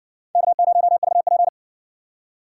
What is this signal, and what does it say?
Morse code, heard as a keyed tone, about 60 wpm, 700 Hz. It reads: K0VP